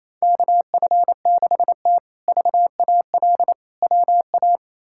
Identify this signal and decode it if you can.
Morse code, heard as a keyed tone, about 28 wpm, 700 Hz. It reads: KF6T VAL WA
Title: K1USN SST QSO Practice - 28wpm